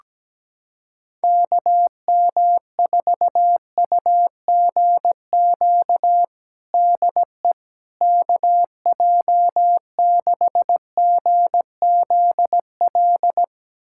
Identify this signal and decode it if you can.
Morse code, heard as a keyed tone, about 17 wpm, 700 Hz. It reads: KM4UGQ DE KJ6GZL
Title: USA Callsign DE Callsign 17wpm